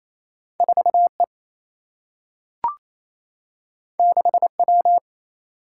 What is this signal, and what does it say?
Morse code, heard as a keyed tone, about 28 wpm, 700 Hz. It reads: 4 E 6 W
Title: Single-Letter Single-Number 28wpm